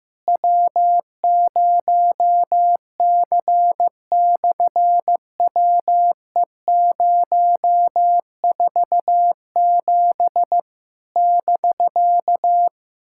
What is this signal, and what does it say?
Morse code, heard as a keyed tone, about 15 wpm, 700 Hz. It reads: W0C/WE047 <BK>
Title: SOTA QSO Practice 15wpm